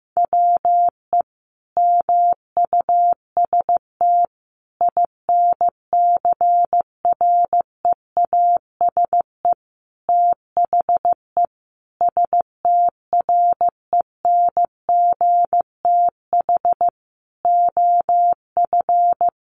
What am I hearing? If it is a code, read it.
Morse code, heard as a keyed tone, about 14 wpm, 700 Hz. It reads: WE MUST INCREASE THE STRENGTH OF
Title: Sentences from Top 1300 Words 15wpm